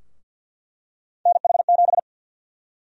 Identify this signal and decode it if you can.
Morse code, heard as a keyed tone, about 47 wpm, 700 Hz. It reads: NH7
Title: USA Call Sign Prefixes - 50wpm